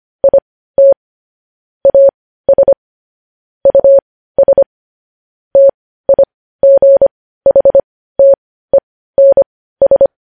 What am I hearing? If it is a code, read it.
Morse code, heard as a keyed tone, about 25 wpm, spaced as for 15 wpm, 550 Hz. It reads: IT AS US TIGHTENS
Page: News Headlines in Morse Code at 15 WPM